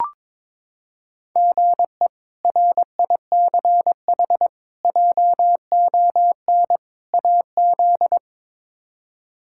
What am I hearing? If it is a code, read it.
Morse code, heard as a keyed tone, about 21 wpm, 700 Hz. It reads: GE RICH JON AZ
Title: K1USN SST QSO Practice - 22wpm